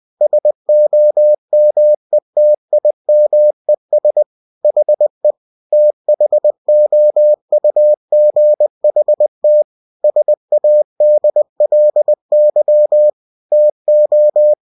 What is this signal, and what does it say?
Morse code, heard as a keyed tone, about 20 wpm, 600 Hz. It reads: SOMETIMES HE THOUGHT SADLY TO
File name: cw-books / winnie-the-pooh / winnie-20wpm-page-0026